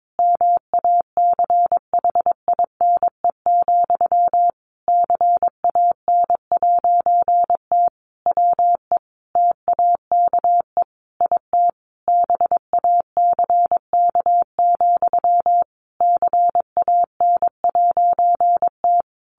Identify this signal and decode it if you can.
Morse code, heard as a keyed tone, about 22 wpm, 700 Hz. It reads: MACHINE, CAN'T WE TAKE IT BACK, CAN'T
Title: A Sound Of Thunder 22wpm